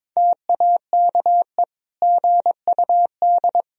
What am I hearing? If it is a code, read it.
Morse code, heard as a keyed tone, about 22 wpm, 700 Hz. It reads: TAKE GUD
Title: Sentences from Top 400 Words and Abbreviations 22wpm